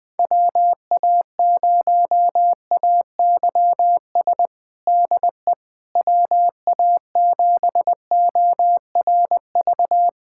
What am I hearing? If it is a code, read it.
Morse code, heard as a keyed tone, about 21 wpm, 700 Hz. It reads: WA0AYS DE WA7ORV
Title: USA Callsign DE Callsign - 6 Characters - 20wpm